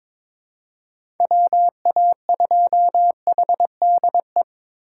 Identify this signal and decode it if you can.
Morse code, heard as a keyed tone, about 21 wpm, 700 Hz. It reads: WA2HDE
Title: USA Call Signs - 6 Characters - 3x Character Spacing - 22wpm